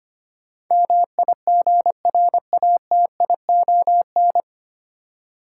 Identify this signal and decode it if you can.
Morse code, heard as a keyed tone, about 25 wpm, 700 Hz. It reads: MIGRATION
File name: practice / 25wpm / 9 Letter Words - First 3 Letters Given - 25wpm